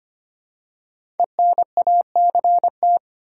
Morse code, heard as a keyed tone, about 25 wpm, 700 Hz. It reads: ENACT
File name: 5 Letter Words - First 3 Letters Given - 25wpm